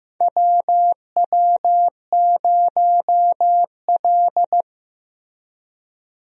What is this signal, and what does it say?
Morse code, heard as a keyed tone, about 16 wpm, 700 Hz. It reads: WW0L
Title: USA Call Signs - 4 Characters - Repeated 2x - 15wpm